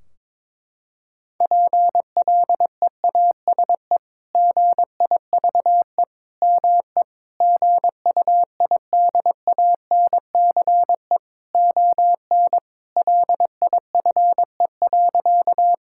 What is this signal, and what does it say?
Morse code, heard as a keyed tone, about 21 wpm, 700 Hz. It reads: PLEASE GIVE ME GUIDANCE ON LIFE.
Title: Sentences from Top 1600 Words 22wpm